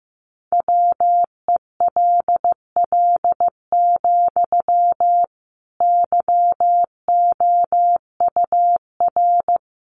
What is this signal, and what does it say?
Morse code, heard as a keyed tone, about 15 wpm, 700 Hz. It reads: WELL, YOUR
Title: Sentences from Top 1400 Words 15wpm